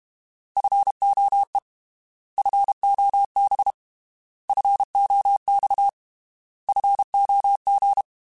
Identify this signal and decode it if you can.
Morse code, heard as a keyed tone, about 32 wpm, 800 Hz. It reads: FOE FOB FOX FOG
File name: Alphabetical / Alphabetical mp3s / Fs @30